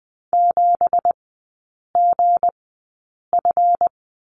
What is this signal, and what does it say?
Morse code, heard as a keyed tone, about 20 wpm, spaced as for 8 wpm, 690 Hz. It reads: 7GF
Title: 205 Lesson 33e - Letter j - Three Characters 20wpm